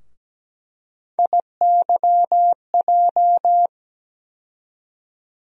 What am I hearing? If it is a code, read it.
Morse code, heard as a keyed tone, about 17 wpm, 700 Hz. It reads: IYJ
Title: USA Call Sign Postfixes - 3 Letters 17wpm